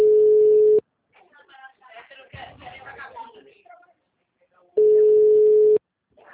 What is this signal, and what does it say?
Ringback tone (Europe, 425 Hz), with speech.
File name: out- -2015- - - .3092